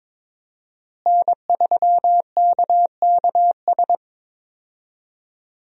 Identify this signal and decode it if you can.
Morse code, heard as a keyed tone, about 22 wpm, 700 Hz. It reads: N3KKS
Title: USA Call Signs - 5 Characters - Repeated 3x - 22wpm